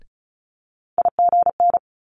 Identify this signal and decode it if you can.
Morse code, heard as a keyed tone, about 36 wpm, 700 Hz. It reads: IGN
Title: Multi-letter Phonemes Quick Fire 35wpm